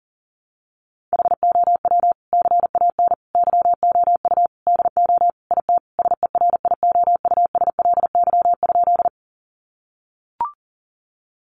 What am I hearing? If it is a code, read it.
Morse code, heard as a keyed tone, about 40 wpm, 700 Hz. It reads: HOW CAN YOU DO IT SERIOUSLY?
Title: Sentences from Top 1200 Words 40wpm